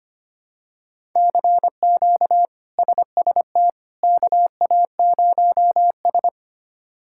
Morse code, heard as a keyed tone, about 25 wpm, 700 Hz. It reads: CQ SST KA0S
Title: K1USN SST QSO Practice - 25wpm